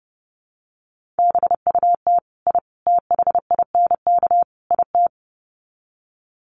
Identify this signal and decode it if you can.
Morse code, heard as a keyed tone, about 30 wpm, 700 Hz. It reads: BUT I THINK IT
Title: Sets of 4 Words 30wpm